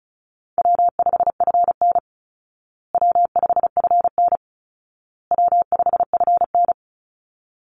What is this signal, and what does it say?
Morse code, heard as a keyed tone, about 34 wpm, 700 Hz. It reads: W5FN W5FN W5FN
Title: Call Signs Repeated 2x - 35wpm